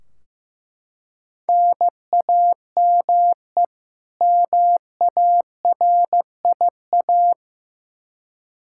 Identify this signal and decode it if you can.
Morse code, heard as a keyed tone, about 15 wpm, 700 Hz. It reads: NAME MARIA
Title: Practice QSO - Exchange 1 - Beginning - 15wpm